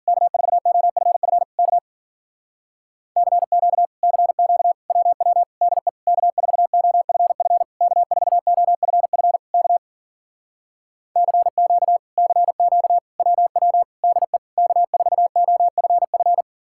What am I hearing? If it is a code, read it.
Morse code, heard as a keyed tone, about 51 wpm, 700 Hz. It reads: K4OFF K CQ CQ WW DE K4OFF K4OFF K CQ CQ WW DE K4OFF